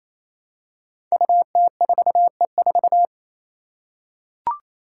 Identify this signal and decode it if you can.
Morse code, heard as a keyed tone, about 28 wpm, 700 Hz. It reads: UT4E4
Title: Five Challenging Letter-Number Combinations 28wpm